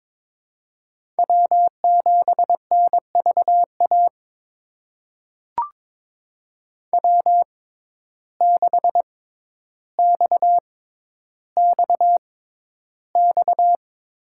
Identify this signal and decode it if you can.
Morse code, heard as a keyed tone, about 22 wpm, spaced as for 11 wpm, 700 Hz. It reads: W7NVA W6XXX
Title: USA Call Signs - 5 Characters - 6x Character Spacing - 22wpm